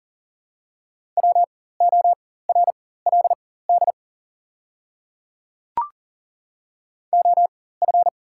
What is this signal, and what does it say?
Morse code, heard as a keyed tone, about 40 wpm, spaced as for 19 wpm, 690 Hz. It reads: WORLD OF
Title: 4x Character Spacing - Top 200 Words - 40wpm